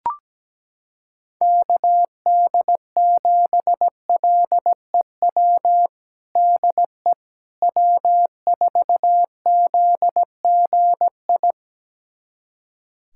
Morse code, heard as a keyed tone, about 17 wpm, 700 Hz. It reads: KD7LEW DE W4ZGI
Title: USA Callsign DE Callsign 17wpm